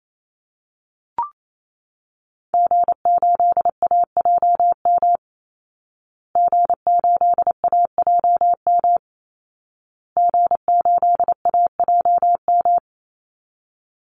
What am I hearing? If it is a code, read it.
Morse code, heard as a keyed tone, about 28 wpm, 700 Hz. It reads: G8AJM G8AJM G8AJM
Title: Call Signs Repeated 2x - 28wpm